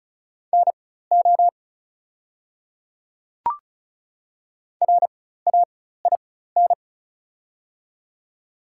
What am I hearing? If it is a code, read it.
Morse code, heard as a keyed tone, about 36 wpm, spaced as for 16 wpm, 690 Hz. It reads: NO RAIN
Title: 4x Character Spacing - Top 300 Words - 35wpm